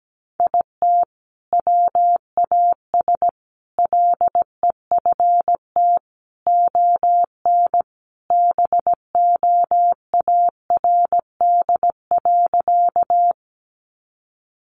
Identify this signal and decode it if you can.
Morse code, heard as a keyed tone, about 17 wpm, 700 Hz. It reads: IT WAS LEFT ON BOARD.
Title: Sentences from Top 700 Words 17wpm